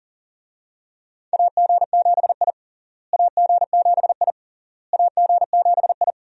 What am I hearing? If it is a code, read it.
Morse code, heard as a keyed tone, about 37 wpm, 700 Hz. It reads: AG7I AG7I AG7I
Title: USA Call Signs - 4 Characters - Repeated 3x - 40wpm